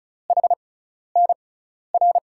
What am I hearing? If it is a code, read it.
Morse code, heard as a keyed tone, about 34 wpm, spaced as for 12 wpm, 700 Hz. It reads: HNR
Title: USA Call Signs - 6x Character Spacing - 35wpm